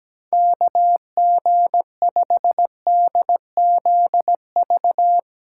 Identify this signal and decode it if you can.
Morse code, heard as a keyed tone, about 17 wpm, 700 Hz. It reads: KG5DZV
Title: USA Call Signs - 6 Characters - 3x Character Spacing - 17wpm